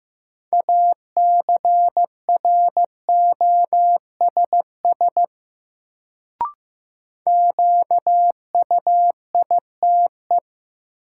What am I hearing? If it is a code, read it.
Morse code, heard as a keyed tone, about 15 wpm, 700 Hz. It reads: ACROSS QUITE
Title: Sentences from Top 700 Words - New Word Drill - 15wpm